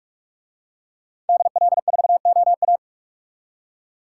Morse code, heard as a keyed tone, about 49 wpm, 700 Hz. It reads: DL4OA